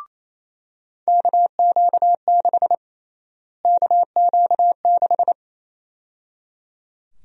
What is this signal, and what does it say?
Morse code, heard as a keyed tone, about 28 wpm, 700 Hz. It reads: KQ6 KQ6
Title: USA Call Sign Prefixes - Repeated 2x - 28wpm